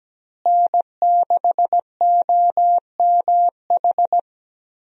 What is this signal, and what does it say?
Morse code, heard as a keyed tone, about 17 wpm, 700 Hz. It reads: N6OMH
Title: USA Call Signs - 5 Characters - Repeated 3x - 17wpm